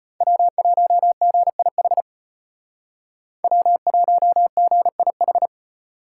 Morse code, heard as a keyed tone, about 32 wpm, 700 Hz. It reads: W1GIH W1GIH